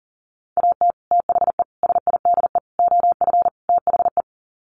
Morse code, heard as a keyed tone, about 40 wpm, 700 Hz. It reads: AT THE SIDE OF THE
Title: Sets of 5 Words Encore2 40wpm